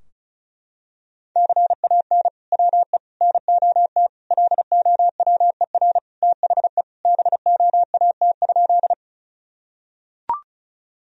Morse code, heard as a keyed tone, about 34 wpm, 700 Hz. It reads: CAN WE NOT LOWER THE BOAT?
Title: Sentences from Top 1000 Words - Review Part 4 - 35wpm